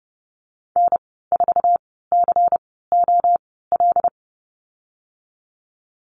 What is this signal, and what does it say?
Morse code, heard as a keyed tone, about 32 wpm, spaced as for 17 wpm, 700 Hz. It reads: N4COL
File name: Lesson 19g - Letter w - Call Signs 30wpm